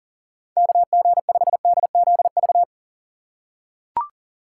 Morse code, heard as a keyed tone, about 40 wpm, 700 Hz. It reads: KG5DZV
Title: USA Call Signs - 6 Characters - 4x Character Spacing - 40wpm